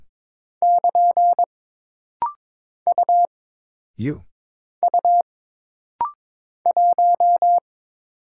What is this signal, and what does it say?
Morse code, heard as a keyed tone, about 22 wpm, 700 Hz. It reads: <KN> U U 1
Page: This rapid-fire timing is geared to push your ICR (Instant Character Recognition) skills.